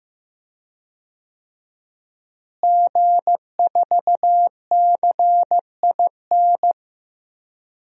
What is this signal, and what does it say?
Morse code, heard as a keyed tone, about 15 wpm, 700 Hz. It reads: G4CIN
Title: DXCC-100 Call Signs - 5 Characters - Repeated 2x - 15wpm